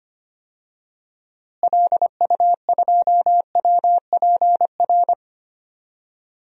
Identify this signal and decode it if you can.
Morse code, heard as a keyed tone, about 25 wpm, 700 Hz. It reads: LU2WPR
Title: DXCC-100 Call Signs - 6 Characters - Repeated 2x - 25wpm